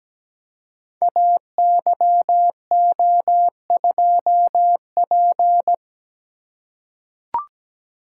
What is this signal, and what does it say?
Morse code, heard as a keyed tone, about 17 wpm, 700 Hz. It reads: AYO2P
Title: Code Groups - 5 Letter:Numbers - 17wpm